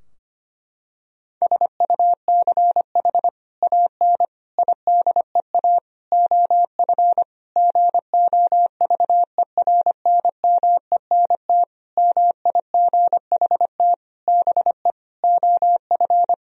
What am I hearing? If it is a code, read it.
Morse code, heard as a keyed tone, about 25 wpm, 700 Hz. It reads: SUCH AN IDEA OF GOVERNMENT MIGHT BE OF
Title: Sentences from Top 1000 Words - Review Part 3 - 25wpm